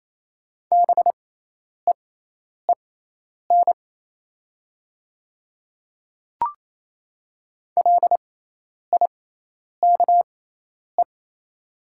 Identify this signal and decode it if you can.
Morse code, heard as a keyed tone, about 28 wpm, spaced as for 10 wpm, 690 Hz. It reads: BEEN LIKE
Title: 6x Character Spacing - Top 200 Words - 28wpm